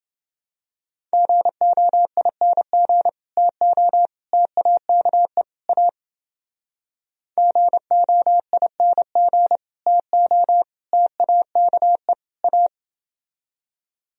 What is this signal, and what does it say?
Morse code, heard as a keyed tone, about 30 wpm, 700 Hz. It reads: GOING TO TAKE A GOING TO TAKE A